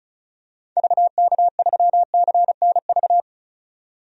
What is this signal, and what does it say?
Morse code, heard as a keyed tone, about 34 wpm, 700 Hz. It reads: VK3CNV
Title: DXCC-100 Call Signs - 6 Characters - 35wpm